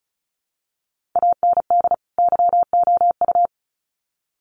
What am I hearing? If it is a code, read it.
Morse code, heard as a keyed tone, about 34 wpm, 700 Hz. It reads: AND YOU
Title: Sets of 2 Words 35wpm